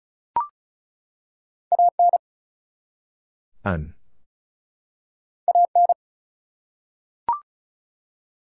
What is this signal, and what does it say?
Morse code, heard as a keyed tone, about 33 wpm, 700 Hz. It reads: AN AN
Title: Two Letter Words - 35wpm